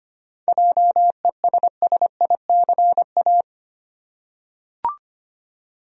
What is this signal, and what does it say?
Morse code, heard as a keyed tone, about 25 wpm, 700 Hz. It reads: JESSICA
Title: USA Common Names - 25wpm